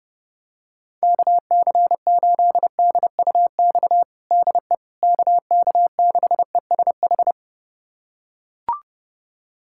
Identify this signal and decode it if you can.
Morse code, heard as a keyed tone, about 32 wpm, 700 Hz. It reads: KC8DUX DE KK6ESH
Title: USA Callsign DE Callsign 30wpm